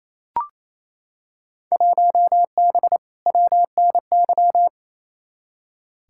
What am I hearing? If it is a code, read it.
Morse code, heard as a keyed tone, about 28 wpm, 700 Hz. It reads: 1B WNY
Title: Field Day Exchanges Encore - 28wpm